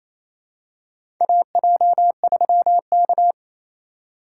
Morse code, heard as a keyed tone, about 28 wpm, 700 Hz. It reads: AJ3K
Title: USA Call Signs - 2x Character Spacing - 28wpm